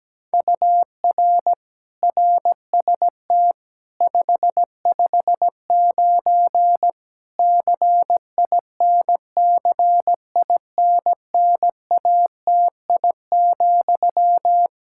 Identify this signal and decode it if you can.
Morse code, heard as a keyed tone, about 17 wpm, 700 Hz. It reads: UR RST 559 CINCINNATI,
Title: Practice QSO - Exchange 1 - Beginning - 17wpm